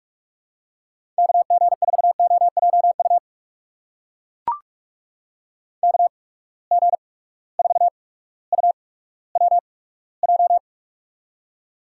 Morse code, heard as a keyed tone, about 44 wpm, spaced as for 19 wpm, 700 Hz. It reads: KG4OJU KG4UWJ
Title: USA Call Signs - 6 Characters - 8x Character Spacing - 45wpm